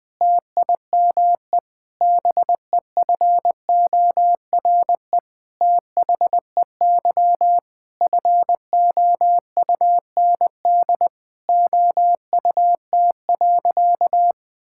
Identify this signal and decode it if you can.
Morse code, heard as a keyed tone, about 20 wpm, 700 Hz. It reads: TIME BEFORE THEY FOUND OUT.
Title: Sentences with 8-Word Phrases - 20wpm